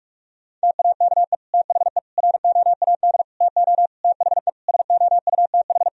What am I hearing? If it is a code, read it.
Morse code, heard as a keyed tone, about 46 wpm, 700 Hz. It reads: TAKE THE ROAD TO THE SOUTH
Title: Sentences from Top 400 Words and Abbreviations 45wpm